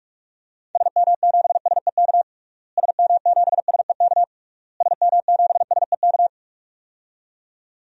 Morse code, heard as a keyed tone, about 41 wpm, 700 Hz. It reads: SM7SEK SM7SEK SM7SEK
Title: DXCC-100 Call Signs - 6 Characters - Repeated 3x - 45wpm